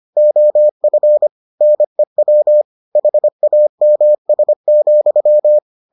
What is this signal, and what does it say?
Morse code, heard as a keyed tone, about 25 wpm, 600 Hz. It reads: OF NEW HAMS,